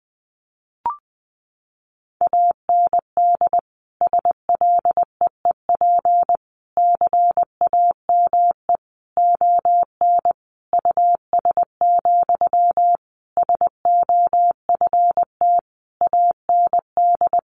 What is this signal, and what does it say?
Morse code, heard as a keyed tone, about 20 wpm, 700 Hz. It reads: AND SLEEP CAME ON US, SOFT AND
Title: Sentences from Top 1500 Words 20wpm